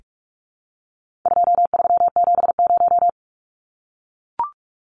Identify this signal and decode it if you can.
Morse code, heard as a keyed tone, about 44 wpm, 700 Hz. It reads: 2370
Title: Numbers - 4-Digit 45wpm